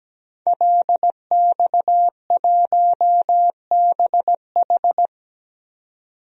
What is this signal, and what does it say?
Morse code, heard as a keyed tone, about 17 wpm, 700 Hz. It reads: LX1BH
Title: DXCC-100 Call Signs - 5 Characters - 17wpm